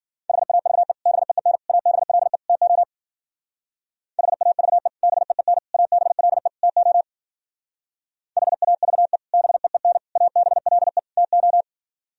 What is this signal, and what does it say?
Morse code, heard as a keyed tone, about 51 wpm, 700 Hz. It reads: HAVE BEEN ABLE TO HAVE BEEN ABLE TO HAVE BEEN ABLE TO